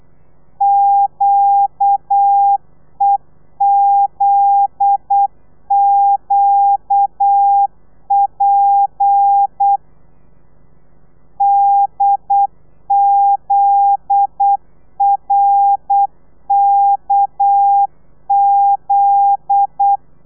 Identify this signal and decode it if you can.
Morse code, heard as a keyed tone, about 8 wpm, 800 Hz. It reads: QEZQP DZRKZ
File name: telegrafia / i2tfj / Lez9 35